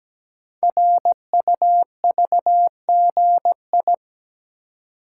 Morse code, heard as a keyed tone, about 17 wpm, 700 Hz. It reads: RUVGI
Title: Code Groups - 5 Letter:Numbers - 17wpm